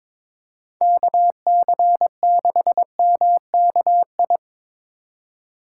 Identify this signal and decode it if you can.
Morse code, heard as a keyed tone, about 22 wpm, 700 Hz. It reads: KC6MKI
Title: USA Call Signs - 6 Characters - 4x Character Spacing - 22wpm